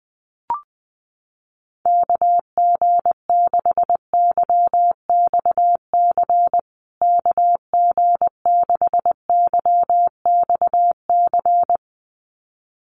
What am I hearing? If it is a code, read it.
Morse code, heard as a keyed tone, about 20 wpm, 700 Hz. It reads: KG6YXC KG6YXC
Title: USA Call Signs Repeated 2x 20wpm